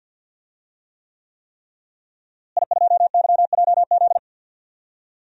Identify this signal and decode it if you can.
Morse code, heard as a keyed tone, about 50 wpm, 700 Hz. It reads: I2YJZ
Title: DXCC-100 Call Signs - 5 Characters - Repeated 3x - 50wpm